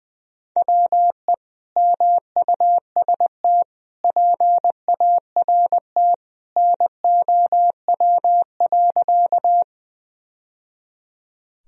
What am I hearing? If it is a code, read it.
Morse code, heard as a keyed tone, about 20 wpm, 700 Hz. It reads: WE MUST PART NOW.
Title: Sentences from Top 1000 Words - Review Part 1 - 20wpm